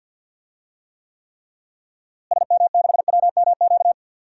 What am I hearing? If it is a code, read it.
Morse code, heard as a keyed tone, about 48 wpm, 700 Hz. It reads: SM6WMQ
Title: DXCC-100 Call Signs - 6 Characters - Repeated 3x - 50wpm